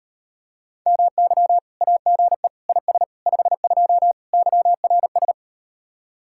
Morse code, heard as a keyed tone, about 38 wpm, 700 Hz. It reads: MY AGE IS 52 YRS